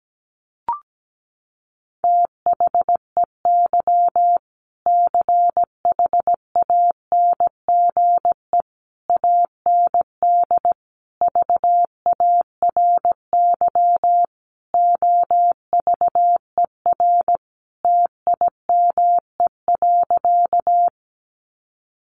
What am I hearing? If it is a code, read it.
Morse code, heard as a keyed tone, about 17 wpm, 700 Hz. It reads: THEY CHANGE AND VARY OVER TIME.
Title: Sentences from Top 1200 Words 17wpm